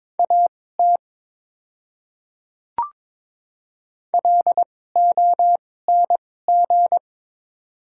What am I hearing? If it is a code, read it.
Morse code, heard as a keyed tone, about 22 wpm, spaced as for 16 wpm, 700 Hz. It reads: AT LONG
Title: 2x Character Spacing - Top 100 Words - 22wpm